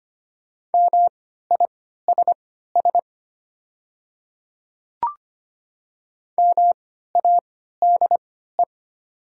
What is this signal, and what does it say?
Morse code, heard as a keyed tone, about 25 wpm, spaced as for 14 wpm, 700 Hz. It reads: MISS MADE
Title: 3x Character Spacing - Top 500 Words - 25wpm